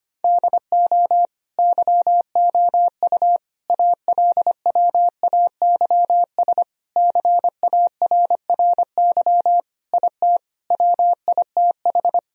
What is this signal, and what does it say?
Morse code, heard as a keyed tone, about 25 wpm, 700 Hz. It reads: DO YOU ALWAYS CARRY IT WITH
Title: Sentences from Top 1000 Words - Review Part 1 - 25wpm